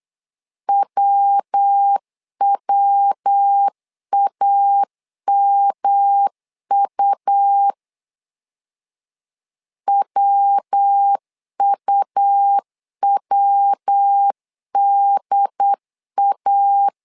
Morse code, heard as a keyed tone, about 8 wpm, 800 Hz. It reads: WWAMU WUWDA
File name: LEZIONI / 3 MWRO / 3 MW 8wpm 2501